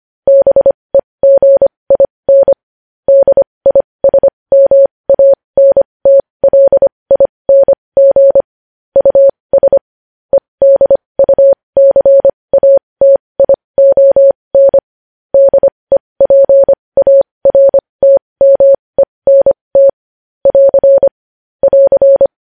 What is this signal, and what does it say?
Morse code, heard as a keyed tone, about 25 wpm, spaced as for 20 wpm, 550 Hz. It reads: BEGIN DISMANTLING US EDUCATION DEPARTMENT <AR> <AR>
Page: news-headlines-in-morse-code-at-20-wpm